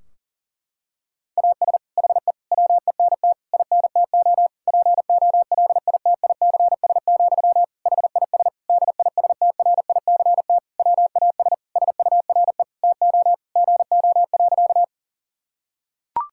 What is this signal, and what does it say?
Morse code, heard as a keyed tone, about 40 wpm, 700 Hz. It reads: AS HE WENT INTO POLITICS, HIS DISTRICT WAS SURE TO GO.
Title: Sentences from Top 1600 Words 40wpm